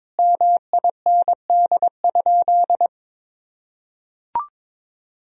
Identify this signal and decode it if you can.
Morse code, heard as a keyed tone, about 22 wpm, 700 Hz. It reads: MIND?
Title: Sentences from Top 1000 Words - Review Part 2 - 22wpm